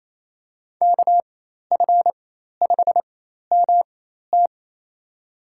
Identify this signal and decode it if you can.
Morse code, heard as a keyed tone, about 28 wpm, spaced as for 13 wpm, 700 Hz. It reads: KF5MT
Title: USA Call Signs - 5 Characters - 4x Character Spacing - 28wpm